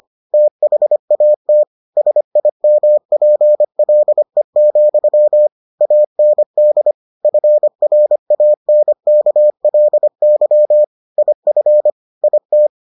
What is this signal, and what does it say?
Morse code, heard as a keyed tone, about 25 wpm, 600 Hz. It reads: THAT SIMPLE, AND FRANKLY IF IT